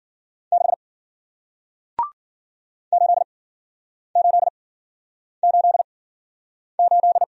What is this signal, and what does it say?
Morse code, heard as a keyed tone, about 53 wpm, spaced as for 9 wpm, 700 Hz. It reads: 6 8888